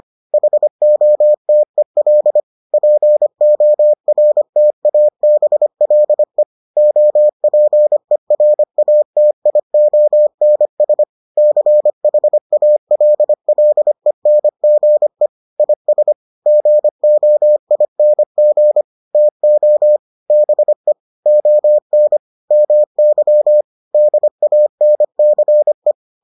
Morse code, heard as a keyed tone, about 25 wpm, 600 Hz. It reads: HOTEL PORTABLE OPERATIONS CHALLENGE IS GOING TO BE ON MY DANCE